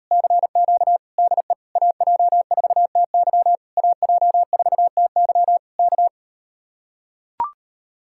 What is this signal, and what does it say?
Morse code, heard as a keyed tone, about 38 wpm, 700 Hz. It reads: CQ DE AJ4TY AJ4TY K